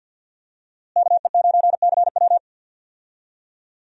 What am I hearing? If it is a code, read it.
Morse code, heard as a keyed tone, about 51 wpm, 700 Hz. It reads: KE9CW
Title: USA Call Signs - 5 Characters - Repeated 3x - 50wpm